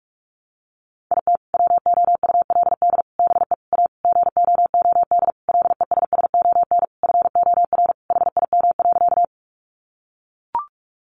Morse code, heard as a keyed tone, about 44 wpm, 700 Hz. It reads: IT WOULD BE A GOOD LESSON FOR HIM.
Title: Sentences from Top 1200 Words 45wpm